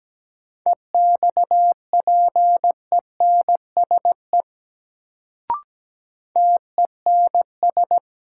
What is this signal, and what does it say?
Morse code, heard as a keyed tone, about 17 wpm, 700 Hz. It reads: EXPENSE TENS
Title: Sets of 4 Words - New Word Drill - 17wpm